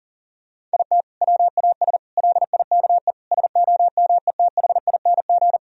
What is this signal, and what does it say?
Morse code, heard as a keyed tone, about 40 wpm, 700 Hz. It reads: IT WAS LIKE SOMETHING
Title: Sentences from Top 1000 Words - Review Part 1 - 40wpm